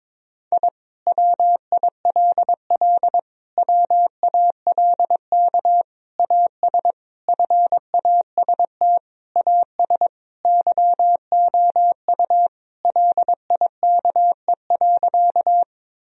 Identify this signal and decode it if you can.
Morse code, heard as a keyed tone, about 21 wpm, 700 Hz. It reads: I WILL WALK AS FAST AS YOU LIKE.
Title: Sentences from Top 1000 Words - Review Part 2 - 22wpm